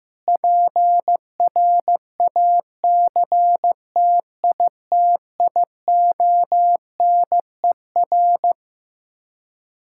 Morse code, heard as a keyed tone, about 15 wpm, 700 Hz. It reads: PRACTITIONER
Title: K1USN SST QSO Practice 15wpm